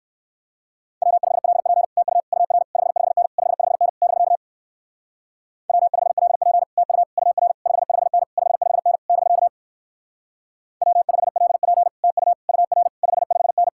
Morse code, heard as a keyed tone, about 56 wpm, 700 Hz. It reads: W5LP TU UR 55N 55N <BK> W5LP TU UR 55N 55N <BK> W5LP TU UR 55N
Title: Speed-Racing Common QSO Phrases - 45wpm